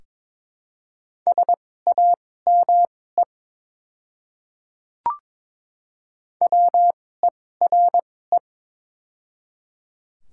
Morse code, heard as a keyed tone, about 22 wpm, spaced as for 16 wpm, 700 Hz. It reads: SAME WERE
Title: 2x Character Spacing - Top 500 Words - 22wpm